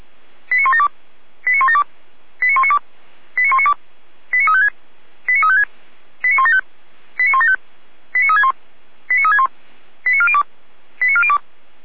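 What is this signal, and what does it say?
Five-tone selective call (ZVEI), heard as quick runs of five tones, tones of about 70 ms. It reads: 89172 89172 89192 89192 89356 89356 89175 89175 89361 89361 89402 89402
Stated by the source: ZVEI five-tone selective call received by SDR.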